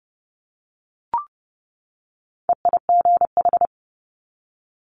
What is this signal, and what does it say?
Morse code, heard as a keyed tone, about 30 wpm, 700 Hz. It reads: EIGH
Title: Multi-letter Phonemes 30wpm